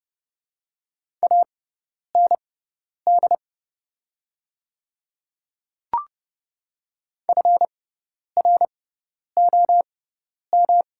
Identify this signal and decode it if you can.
Morse code, heard as a keyed tone, about 30 wpm, spaced as for 10 wpm, 690 Hz. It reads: AND FROM
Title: 6x Character Spacing - Top 300 Words - 30wpm